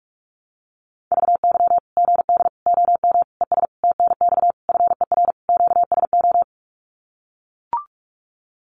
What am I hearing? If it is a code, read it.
Morse code, heard as a keyed tone, about 44 wpm, 700 Hz. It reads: VY GD OM ES TNX FER QSO
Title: Common QSO Phrases 45wpm